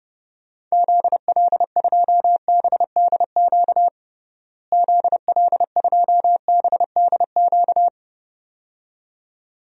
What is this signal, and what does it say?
Morse code, heard as a keyed tone, about 29 wpm, 700 Hz. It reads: ZL2BDQ ZL2BDQ
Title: DXCC-100 Call Signs - 6 Characters - Repeated 2x - 30wpm